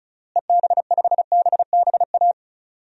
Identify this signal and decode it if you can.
Morse code, heard as a keyed tone, about 36 wpm, 700 Hz. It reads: EB5BBA
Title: DXCC-100 Call Signs - 6 Characters - Repeated 3x - 35wpm